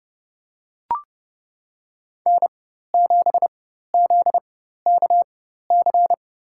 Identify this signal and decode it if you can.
Morse code, heard as a keyed tone, about 31 wpm, spaced as for 14 wpm, 700 Hz. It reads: N7ZKC
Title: USA Call Signs - 4x Character Spacing - 30wpm